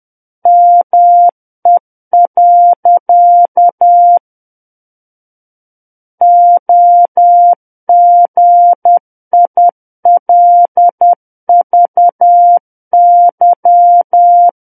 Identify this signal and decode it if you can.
Morse code, heard as a keyed tone, about 10 wpm, 700 Hz. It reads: ME. OGILVY
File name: Book One / 01-Chapter 1 - 10 WPM